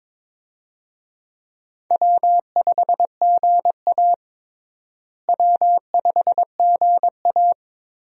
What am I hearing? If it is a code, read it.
Morse code, heard as a keyed tone, about 22 wpm, 700 Hz. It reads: W5GA W5GA
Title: USA Call Signs - 4 Characters - Repeated 2x - 22wpm